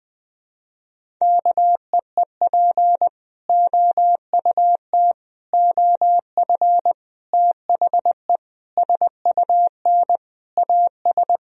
Morse code, heard as a keyed tone, about 20 wpm, 700 Hz. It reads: KEEP OUT OF THE SUN AS
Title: Sentences from Top 1000 Words - Review Part 1 - 20wpm